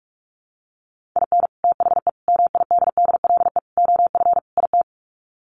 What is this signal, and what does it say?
Morse code, heard as a keyed tone, about 46 wpm, 700 Hz. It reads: IN THE MIDDLE OF IT
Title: Sets of 5 Words Encore 45wpm